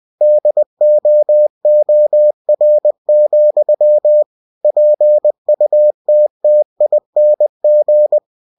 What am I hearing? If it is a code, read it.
Morse code, heard as a keyed tone, about 20 wpm, 600 Hz. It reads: DOOR, PUTTING
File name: cw-books / winnie-the-pooh / winnie-20wpm-page-0076